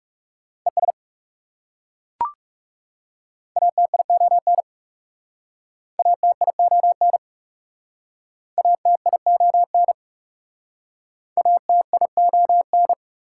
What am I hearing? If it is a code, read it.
Morse code, heard as a keyed tone, about 40 wpm, 690 Hz. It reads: ES ATION ATION ATION ATION